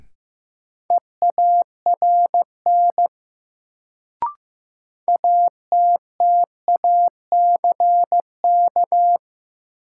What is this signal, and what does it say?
Morse code, heard as a keyed tone, about 15 wpm, 700 Hz. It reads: EARN ATTACK
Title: Sentences from Top 1500 Words - New Word Drill - 15wpm